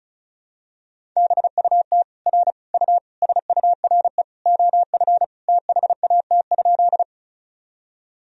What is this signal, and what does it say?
Morse code, heard as a keyed tone, about 36 wpm, 700 Hz. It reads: BUT R U SURE OF THAT?
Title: Sentences from Top 300 Words and Abbreviations 35wpm